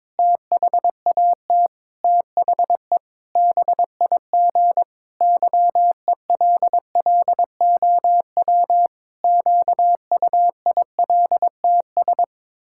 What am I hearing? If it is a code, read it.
Morse code, heard as a keyed tone, about 22 wpm, 700 Hz. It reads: THAT THE BIG YELLOW QUILTS
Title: Pangram Challenge v2 - 22wpm